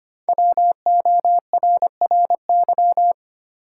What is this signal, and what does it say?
Morse code, heard as a keyed tone, about 25 wpm, 700 Hz. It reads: WORRY
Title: Sentences from Top 1100 Words - New Word Drill - 25wpm